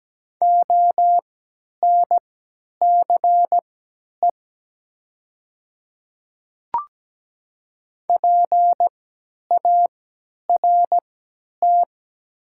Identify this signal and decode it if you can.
Morse code, heard as a keyed tone, about 17 wpm, spaced as for 10 wpm, 700 Hz. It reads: ONCE PART
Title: 3x Character Spacing - Top 300 Words - 17wpm